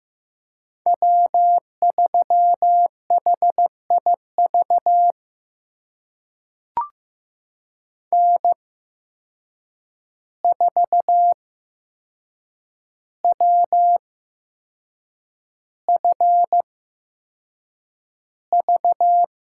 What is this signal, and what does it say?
Morse code, heard as a keyed tone, about 15 wpm, spaced as for 6 wpm, 700 Hz. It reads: W3HIV N4WFV
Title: USA Call Signs - 5 Characters - 8x Character Spacing - 15wpm